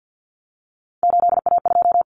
Morse code, heard as a keyed tone, about 50 wpm, 700 Hz. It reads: 8A2
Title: Common Cut Numbers - 3-Digit 50wpm